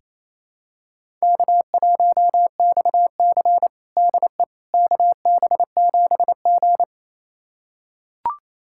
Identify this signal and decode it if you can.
Morse code, heard as a keyed tone, about 28 wpm, 700 Hz. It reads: K1XC DE KB7G
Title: USA Callsign DE Callsign - 4 Characters - 28wpm